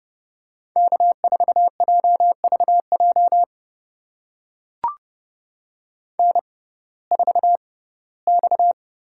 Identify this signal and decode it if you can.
Morse code, heard as a keyed tone, about 30 wpm, spaced as for 18 wpm, 700 Hz. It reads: K4JVJ N4X
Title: USA Call Signs - 6x Character Spacing - 30wpm